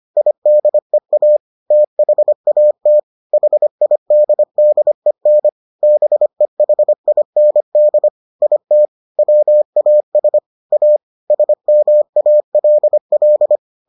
Morse code, heard as a keyed tone, about 25 wpm, 600 Hz. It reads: IDEA THAT HIDDEN BEHIND IT WAS A SMALL